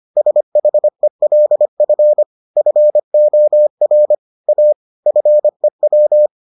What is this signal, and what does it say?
Morse code, heard as a keyed tone, about 25 wpm, 600 Hz. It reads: SHELF FOR A FEW